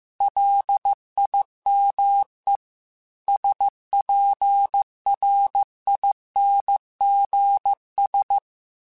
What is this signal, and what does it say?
Morse code, heard as a keyed tone, about 15 wpm, 800 Hz. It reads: LIME SPRINGS